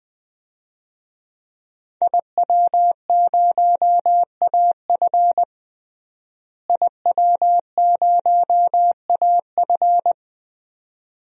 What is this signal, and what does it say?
Morse code, heard as a keyed tone, about 21 wpm, 700 Hz. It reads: IW0AF IW0AF
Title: DXCC-100 Call Signs - 5 Characters - Repeated 2x - 20wpm